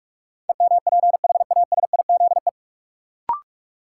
Morse code, heard as a keyed tone, about 46 wpm, 700 Hz. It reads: EMPHASIZE
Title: Sentences from Top 1500 Words - New Word Drill - 45wpm